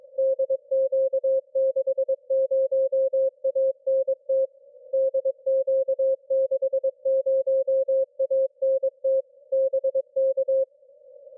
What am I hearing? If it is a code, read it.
Morse code, heard as a keyed tone, about 22 wpm, 530 Hz. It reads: DQ60ANT DQ60ANT BK
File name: DQ60ANT-28MHZ from Arvid in Norway